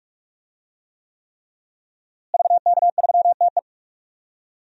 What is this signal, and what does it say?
Morse code, heard as a keyed tone, about 43 wpm, 700 Hz. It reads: VK3TE
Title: DXCC-100 Call Signs - 5 Characters - Repeated 3x - 45wpm